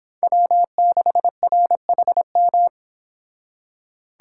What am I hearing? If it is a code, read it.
Morse code, heard as a keyed tone, about 26 wpm, 700 Hz. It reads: W6RHM